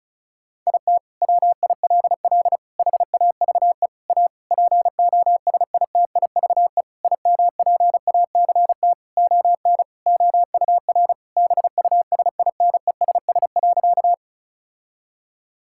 Morse code, heard as a keyed tone, about 36 wpm, 700 Hz. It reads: IT WILL HAVE A POSITIVE IMPACT ON OUR BUSINESS.
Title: Sentences from Top 1000 Words - Review Part 6 - 35wpm